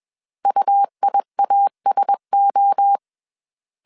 Morse code, heard as a keyed tone, about 21 wpm, 800 Hz. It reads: UIASO
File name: CW / DET / 1 6 1500C S C 20wpm 2116